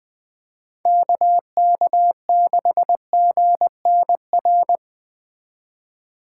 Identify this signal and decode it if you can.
Morse code, heard as a keyed tone, about 19 wpm, 700 Hz. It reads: KK6GNR
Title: Speed Racing - USA 2x3 Call Signs 17wpm